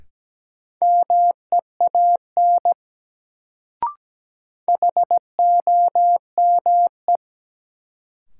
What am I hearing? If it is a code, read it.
Morse code, heard as a keyed tone, about 17 wpm, 700 Hz. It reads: MEAN HOME
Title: Sentences from Top 200 Words - New Word Drill - 17wpm